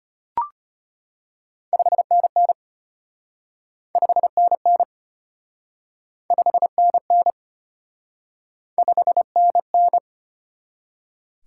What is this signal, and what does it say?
Morse code, heard as a keyed tone, about 34 wpm, 700 Hz. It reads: 5NN 5NN 5NN 5NN